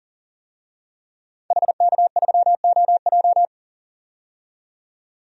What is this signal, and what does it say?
Morse code, heard as a keyed tone, about 38 wpm, 700 Hz. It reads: HK3OJ
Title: DXCC-100 Call Signs - 5 Characters - 40wpm